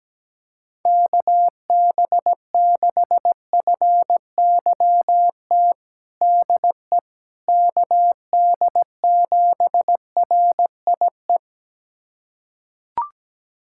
Morse code, heard as a keyed tone, about 17 wpm, 700 Hz. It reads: KB6FYT DE KD7RIE
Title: USA Callsign DE Callsign - 6 Characters - 17wpm